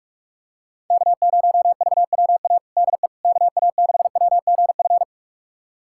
Morse code, heard as a keyed tone, about 46 wpm, 700 Hz. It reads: K0VWA DE KA6WGF
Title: USA Callsign DE Callsign 45wpm